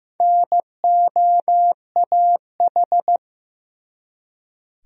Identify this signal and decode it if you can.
Morse code, heard as a keyed tone, about 15 wpm, 700 Hz. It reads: NOAH
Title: USA Common Names - Max 4 Letters - 15wpm